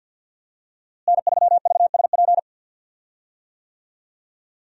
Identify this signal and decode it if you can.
Morse code, heard as a keyed tone, about 48 wpm, 700 Hz. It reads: N3VSP
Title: USA Call Signs - 5 Characters - Repeated 2x - 50wpm